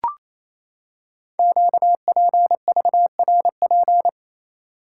Morse code, heard as a keyed tone, about 27 wpm, 700 Hz. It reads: QPVRP
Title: Code Groups - 5 Letters - 28wpm